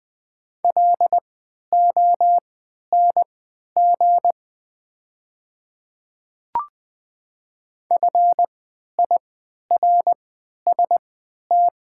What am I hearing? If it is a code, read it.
Morse code, heard as a keyed tone, about 20 wpm, spaced as for 11 wpm, 700 Hz. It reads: LONG FIRST
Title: 3x Character Spacing - Top 100 Words - 20wpm